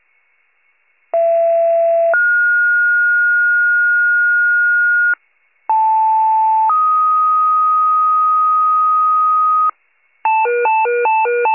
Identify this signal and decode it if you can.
Two-tone page, heard as a one-second tone, then a three-second tone, about 670 then 1430 Hz, about 880 then 1250 Hz, with noise. It is heard from 1.1 to 9.7 s.
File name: wcalert